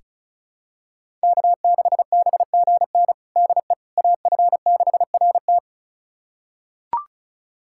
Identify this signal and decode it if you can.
Morse code, heard as a keyed tone, about 36 wpm, 700 Hz. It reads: K6BGN DE AF6RT
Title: USA Callsign DE Callsign - 5 Characters - 35wpm